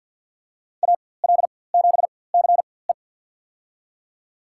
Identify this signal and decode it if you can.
Morse code, heard as a keyed tone, about 53 wpm, spaced as for 24 wpm, 700 Hz. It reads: AL7CE
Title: USA Call Signs - 5 Characters - 4x Character Spacing - 50wpm